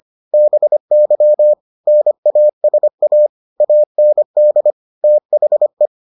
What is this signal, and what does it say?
Morse code, heard as a keyed tone, about 25 wpm, 600 Hz. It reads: BY NASA AND THE